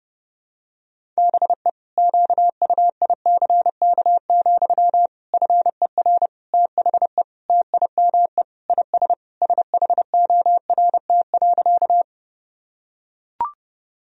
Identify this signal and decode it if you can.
Morse code, heard as a keyed tone, about 30 wpm, 700 Hz. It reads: BE QUICK, FER THE TIME IS SHORT.
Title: Sentences from Top 500 Words and Abbreviations 30wpm